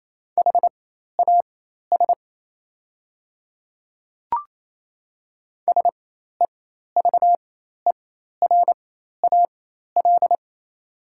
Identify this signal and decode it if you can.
Morse code, heard as a keyed tone, about 28 wpm, spaced as for 13 wpm, 700 Hz. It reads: HAS SEVERAL
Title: 4x Character Spacing - Top 500 Words - 28wpm